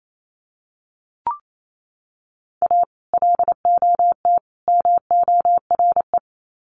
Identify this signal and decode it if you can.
Morse code, heard as a keyed tone, about 28 wpm, 700 Hz. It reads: A LOT MORE
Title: Sets of 3 Words 28wpm